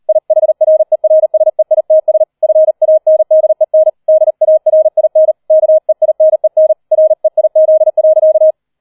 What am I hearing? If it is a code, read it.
Morse code, heard as a keyed tone, about 38 wpm, 630 Hz. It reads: IHRERSEITS FANDEN DARIN KEINEN REIZ.
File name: 40wpm